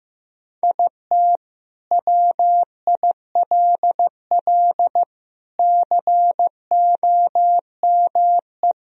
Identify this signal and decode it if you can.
Morse code, heard as a keyed tone, about 15 wpm, 700 Hz. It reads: IT WILL COME
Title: Sentences from Top 1000 Words - Review Part 1 - 15wpm